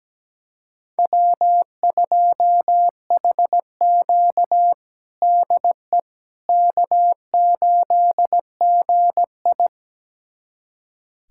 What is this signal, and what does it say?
Morse code, heard as a keyed tone, about 17 wpm, 700 Hz. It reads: W2HQ DE K8GI
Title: USA Callsign DE Callsign - 4 Characters - 17wpm